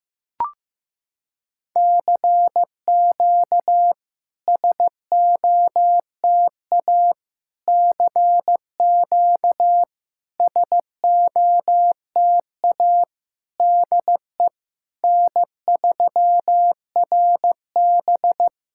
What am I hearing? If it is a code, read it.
Morse code, heard as a keyed tone, about 15 wpm, 700 Hz. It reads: CQ SOTA CQ SOTA DE N3RB
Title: SOTA QSO Practice 15wpm